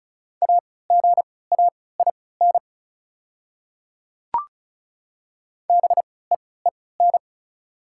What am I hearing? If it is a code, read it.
Morse code, heard as a keyed tone, about 36 wpm, spaced as for 20 wpm, 700 Hz. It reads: AGAIN BEEN
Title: 3x Character Spacing - Top 200 Words - 35wpm